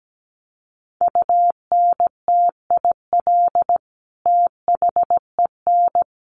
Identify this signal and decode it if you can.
Morse code, heard as a keyed tone, about 17 wpm, 700 Hz. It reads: UNTIL THEN
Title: Sentences from Top 600 Words 17wpm